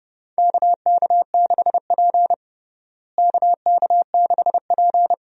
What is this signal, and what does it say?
Morse code, heard as a keyed tone, about 32 wpm, 700 Hz. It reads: KK6P KK6P
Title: USA Call Signs - 4 Characters - Repeated 2x - 30wpm